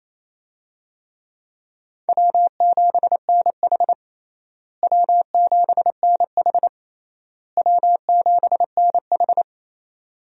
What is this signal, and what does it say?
Morse code, heard as a keyed tone, about 28 wpm, 700 Hz. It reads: W7NH W7NH W7NH
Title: USA Call Signs - 4 Characters - Repeated 3x - 28wpm